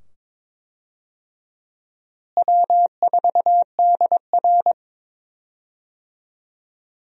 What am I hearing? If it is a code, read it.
Morse code, heard as a keyed tone, about 21 wpm, 700 Hz. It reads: W4DR
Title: USA Call Signs - 4 Characters - Repeated 2x - 22wpm